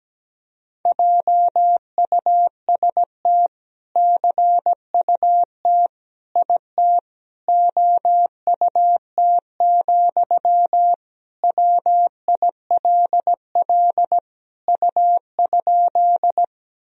Morse code, heard as a keyed tone, about 17 wpm, 700 Hz. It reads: JUST CUT IT OUT, WILL U?
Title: Sentences from Top 300 Words and Abbreviations 17wpm